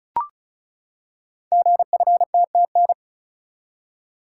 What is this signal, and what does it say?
Morse code, heard as a keyed tone, about 36 wpm, 700 Hz. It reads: GFTTN
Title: Code Groups - 5 Letters - 35wpm